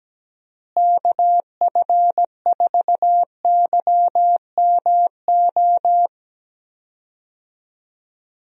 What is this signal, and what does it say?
Morse code, heard as a keyed tone, about 17 wpm, 700 Hz. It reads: KF4YMO
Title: USA Call Signs - 6 Characters - Repeated 3x - 17wpm